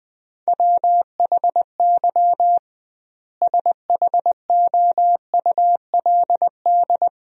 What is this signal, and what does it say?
Morse code, heard as a keyed tone, about 20 wpm, 700 Hz. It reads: WHY SHOULD
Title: 2x Word-Spacing - Sentences from Top 500 Words - 20wpm